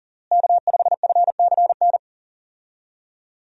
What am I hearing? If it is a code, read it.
Morse code, heard as a keyed tone, about 37 wpm, 700 Hz. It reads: K5FCN
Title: USA Call Signs - 5 Characters - Repeated 3x - 40wpm